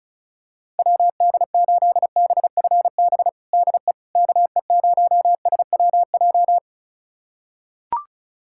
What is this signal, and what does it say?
Morse code, heard as a keyed tone, about 36 wpm, 700 Hz. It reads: WD8BFB DE KE0SWJ
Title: USA Callsign DE Callsign - 6 Characters - 35wpm